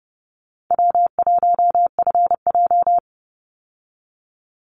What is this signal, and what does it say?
Morse code, heard as a keyed tone, about 30 wpm, 700 Hz. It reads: W1FJ
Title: Easy Call Signs Repeated 2x - 30wpm